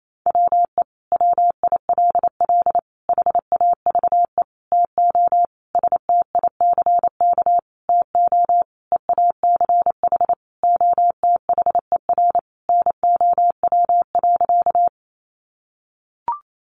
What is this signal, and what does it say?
Morse code, heard as a keyed tone, about 28 wpm, 700 Hz. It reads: WE WILL HAVE TO STICK TO EACH OTHER NOW.
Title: Sentences from Top 1200 Words 28wpm